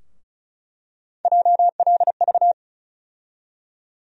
Morse code, heard as a keyed tone, about 36 wpm, 700 Hz. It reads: JLV
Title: USA Call Sign Postfixes - 3 Letters 35wpm